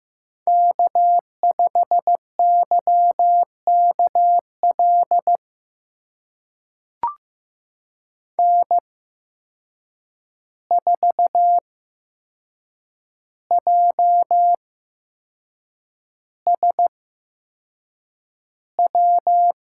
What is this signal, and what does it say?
Morse code, heard as a keyed tone, about 15 wpm, spaced as for 6 wpm, 700 Hz. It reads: K5YKL N4JSW
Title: USA Call Signs - 5 Characters - 8x Character Spacing - 15wpm